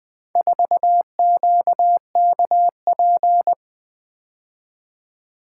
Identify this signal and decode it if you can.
Morse code, heard as a keyed tone, about 20 wpm, 700 Hz. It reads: 4QKP
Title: Four Letter-Number Combinations - 20wpm